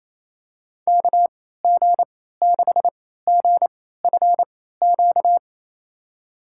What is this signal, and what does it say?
Morse code, heard as a keyed tone, about 27 wpm, spaced as for 16 wpm, 700 Hz. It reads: KG6GFQ
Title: USA Call Signs - 6 Characters - 3x Character Spacing - 28wpm